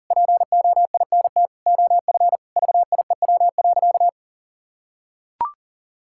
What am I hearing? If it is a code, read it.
Morse code, heard as a keyed tone, about 40 wpm, 700 Hz. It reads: POINT OF VIEW.
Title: Sentences from Top 1000 Words - Review Part 6 - 40wpm